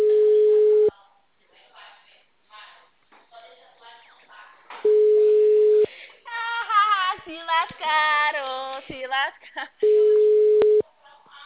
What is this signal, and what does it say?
Ringback tone (Europe, 425 Hz), with speech.